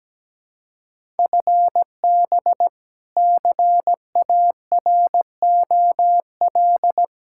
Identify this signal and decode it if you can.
Morse code, heard as a keyed tone, about 17 wpm, 700 Hz. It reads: FB CAROL
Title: Practice QSO - Exchange 3 - More Info - 17wpm